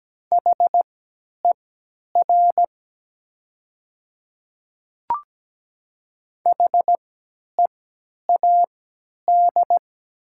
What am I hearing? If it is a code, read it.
Morse code, heard as a keyed tone, about 17 wpm, spaced as for 10 wpm, 700 Hz. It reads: HER HEAD
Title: 3x Character Spacing - Top 300 Words - 17wpm